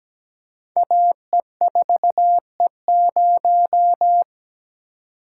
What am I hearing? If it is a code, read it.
Morse code, heard as a keyed tone, about 17 wpm, 700 Hz. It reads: AE4E0
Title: Code Groups - 5 Letter:Numbers - 17wpm